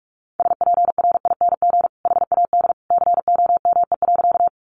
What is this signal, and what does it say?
Morse code, heard as a keyed tone, about 44 wpm, 700 Hz. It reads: SPRING HAD COME.
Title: Sentences from Top 1200 Words 45wpm